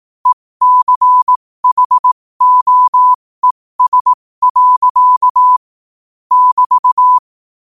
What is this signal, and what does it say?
Morse code, heard as a keyed tone, about 18 wpm, spaced as for 15 wpm, 1000 Hz. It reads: ECHOES. <BT>
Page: New quotes every day in morse code at 15 Words per minute.